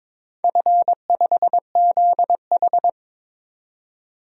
Morse code, heard as a keyed tone, about 22 wpm, 690 Hz. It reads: F5ZH
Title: DXCC-100 Call Signs - 4 Characters - 22wpm